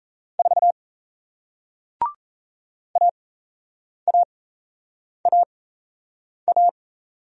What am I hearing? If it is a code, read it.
Morse code, heard as a keyed tone, about 38 wpm, spaced as for 8 wpm, 700 Hz. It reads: 4 AAAA